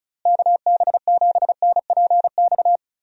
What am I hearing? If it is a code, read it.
Morse code, heard as a keyed tone, about 34 wpm, 700 Hz. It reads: KB7NPX
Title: USA Call Signs - 6 Characters - Repeated 3x - 35wpm